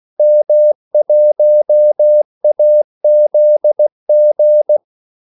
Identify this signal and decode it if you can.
Morse code, heard as a keyed tone, about 16 wpm, 600 Hz. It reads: M1AZG
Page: Kaldesignaler 16 wpm